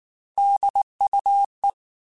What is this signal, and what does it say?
Morse code, heard as a keyed tone, about 19 wpm, 800 Hz. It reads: DUE
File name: Technical Subjects / Technical Subjects-mp3s / AND Gate @18